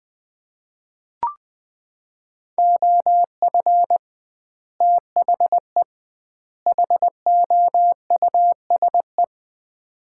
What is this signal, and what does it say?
Morse code, heard as a keyed tone, about 21 wpm, 700 Hz. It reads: OF THE HOUSE
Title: 2x Word-Spacing - Sets of 3 Words - 20wpm